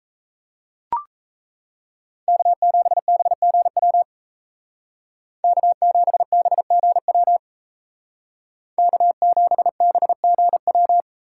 Morse code, heard as a keyed tone, about 36 wpm, 700 Hz. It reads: K7BGW K7BGW K7BGW